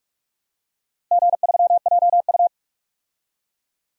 Morse code, heard as a keyed tone, about 42 wpm, 700 Hz. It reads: G3JU
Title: DXCC-100 Call Signs - 4 Characters - Repeated 3x - 45wpm